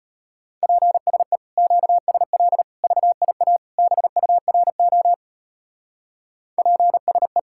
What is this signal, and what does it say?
Morse code, heard as a keyed tone, about 32 wpm, 700 Hz. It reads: PSE QSL VIA BURO PSE